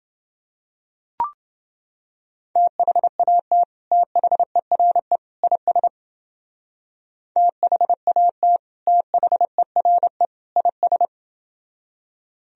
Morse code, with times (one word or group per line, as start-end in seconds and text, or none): 2.6-3.6 THAT
3.9-5.2 THERE
5.4-5.9 IS
7.4-8.6 THAT
8.9-10.2 THERE
10.6-11.1 IS